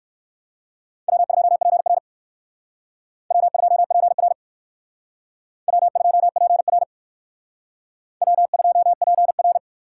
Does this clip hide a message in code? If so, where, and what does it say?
Yes: Morse code from 1.1 to 9.6 s, heard as a keyed tone, about 56 wpm, 700 Hz. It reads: W2PR W2PR W2PR W2PR